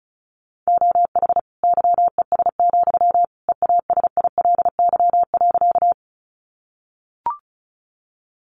Morse code, heard as a keyed tone, about 34 wpm, 700 Hz. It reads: OH YES, EASILY.
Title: Sentences from Top 900 Words 35wpm